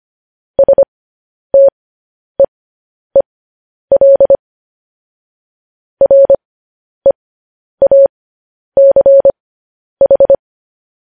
Morse code, heard as a keyed tone, about 25 wpm, spaced as for 10 wpm, 550 Hz. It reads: STEEL REACH
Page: news headlines in morse code at 10 wpm